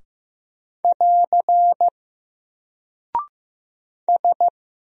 Morse code, heard as a keyed tone, about 15 wpm, 700 Hz. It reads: <AR> S
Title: Single Character (Letter:Number:Punctuation:Prosign) 15wpm